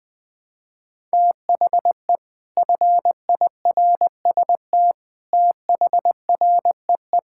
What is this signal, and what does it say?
Morse code, heard as a keyed tone, about 20 wpm, 700 Hz. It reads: THE FIRST THREE
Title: Sentences with 7-Word Phrases - 20wpm